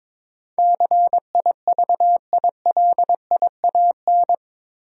Morse code, heard as a keyed tone, about 22 wpm, 700 Hz. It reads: CIVILIAN
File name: practice / 22wpm / Eight Letter Words - 22wpm